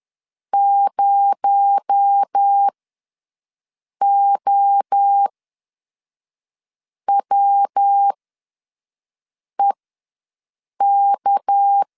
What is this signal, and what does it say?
Morse code, heard as a keyed tone, about 10 wpm, spaced as for 5 wpm, 800 Hz. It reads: 0OWEK
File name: LEZIONI / 7 GF9Z / 7 GF ws 10wpm 3918